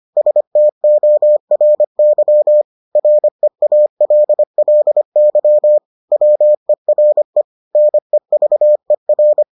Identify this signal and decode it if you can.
Morse code, heard as a keyed tone, about 25 wpm, 600 Hz. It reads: STORY REALLY WERE NEVER